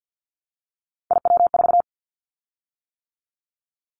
Morse code, heard as a keyed tone, about 50 wpm, 700 Hz. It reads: IW4
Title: Callsign Prefixes for Top 25 DXCC - 50wpm